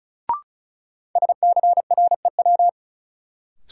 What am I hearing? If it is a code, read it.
Morse code, heard as a keyed tone, about 36 wpm, 700 Hz. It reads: SCREW
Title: Sentences from Top 1600 Words - New Word Drill Alt - 35wpm